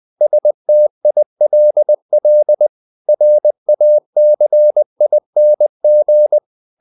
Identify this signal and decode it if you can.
Morse code, heard as a keyed tone, about 20 wpm, 600 Hz. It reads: STILL RACING
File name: the old man and the sea part 2 20wpm